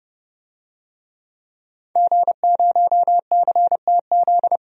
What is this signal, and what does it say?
Morse code, heard as a keyed tone, about 31 wpm, 700 Hz. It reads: G0CTZ
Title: DXCC-100 Call Signs - 5 Characters - 30wpm